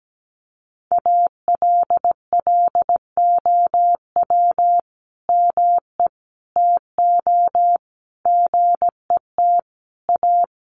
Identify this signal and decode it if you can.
Morse code, heard as a keyed tone, about 17 wpm, 700 Hz. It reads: ALLOW ME TO GET A
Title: Sentences from Top 1300 Words 17wpm